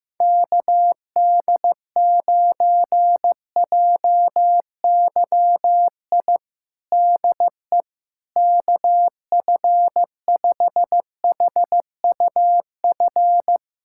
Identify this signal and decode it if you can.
Morse code, heard as a keyed tone, about 15 wpm, 700 Hz. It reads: KD9JYI DE KF5HUF
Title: USA Callsign DE Callsign 15wpm